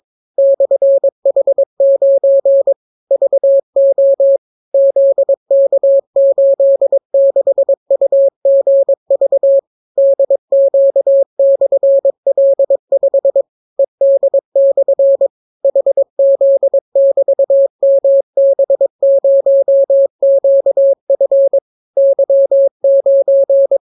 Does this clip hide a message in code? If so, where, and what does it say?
Yes: Morse code from 0.4 to 23.8 s, heard as a keyed tone, about 22 wpm, 550 Hz. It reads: /H9 VO ZK86UGV DQ/L5 ED/ HZ<BT>MB0QF Y9